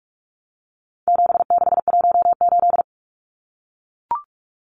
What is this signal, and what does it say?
Morse code, heard as a keyed tone, about 44 wpm, 700 Hz. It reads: 7618
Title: Numbers - 4-Digit 45wpm